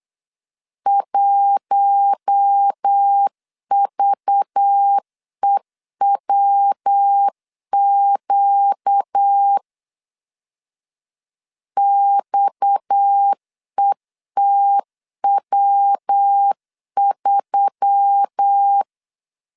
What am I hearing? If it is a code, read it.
Morse code, heard as a keyed tone, about 8 wpm, 800 Hz. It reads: 1VEWQ XETW3
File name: LEZIONI / 7 GF9Z / 7 G 8wpm 3118